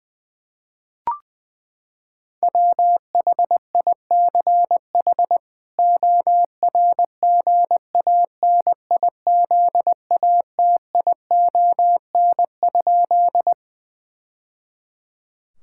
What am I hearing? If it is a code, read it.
Morse code, heard as a keyed tone, about 20 wpm, 700 Hz. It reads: WHICH ORGANIZATION?
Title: Sentences from Top 1000 Words - Review Part 5 - 20wpm